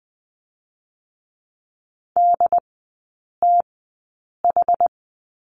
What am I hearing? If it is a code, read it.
Morse code, heard as a keyed tone, about 20 wpm, spaced as for 8 wpm, 700 Hz. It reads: DTH
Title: 093 Lesson 16e - Letter c - Three Characters 20wpm